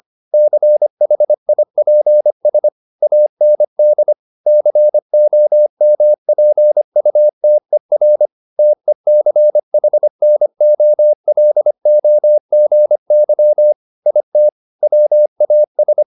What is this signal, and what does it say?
Morse code, heard as a keyed tone, about 25 wpm, 600 Hz. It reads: CHIPS AND COMPUTER TECHNOLOGY IT WAS
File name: SS315 25